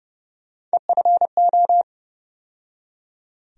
Morse code, heard as a keyed tone, about 30 wpm, 700 Hz. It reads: EFO
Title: USA Call Sign Postfixes - 3 Letters 30wpm